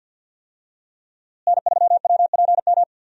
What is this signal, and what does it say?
Morse code, heard as a keyed tone, about 52 wpm, 690 Hz. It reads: N3WPM
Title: USA Call Signs - 5 Characters - Repeated 3x - 50wpm